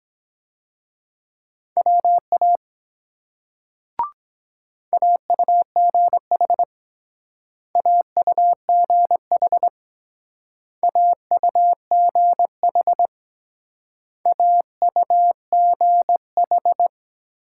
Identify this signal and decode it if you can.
Morse code, heard as a keyed tone, about 22 wpm, 700 Hz. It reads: WA AUGH AUGH AUGH AUGH